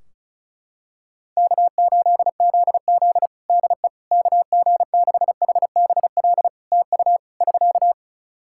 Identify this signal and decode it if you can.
Morse code, heard as a keyed tone, about 36 wpm, 700 Hz. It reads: K8ZZ DE KG6HBL TU <SK>
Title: Practice QSO - Exchange 4 - Ending - 35wpm